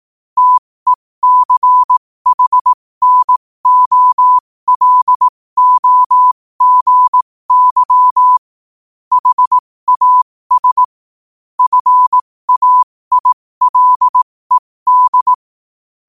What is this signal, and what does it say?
Morse code, heard as a keyed tone, about 18 wpm, spaced as for 15 wpm, 1000 Hz. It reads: TECHNOLOGY HAS FAILED
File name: qotdfiles / 15wpm